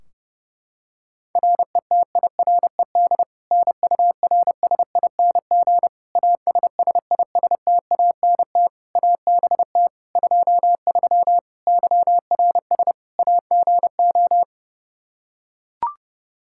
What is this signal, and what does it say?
Morse code, heard as a keyed tone, about 30 wpm, 700 Hz. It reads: RETIRED NURSING ASSISTANT ABT 23 YRS AGO
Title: Practice QSO - Exchange 3 - More Info - 30wpm